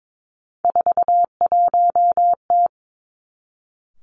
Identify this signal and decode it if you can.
Morse code, heard as a keyed tone, about 22 wpm, 700 Hz. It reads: 41T
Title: Three Challenging Letter-Number Combinations 22wpm